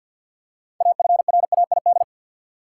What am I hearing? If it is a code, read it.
Morse code, heard as a keyed tone, about 53 wpm, 700 Hz. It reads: AFRAID
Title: Six Letter Words - 50wpm